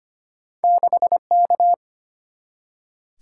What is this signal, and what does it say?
Morse code, heard as a keyed tone, about 25 wpm, 700 Hz. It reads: 6K
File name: Two Letter-Number Combinations - 25wpm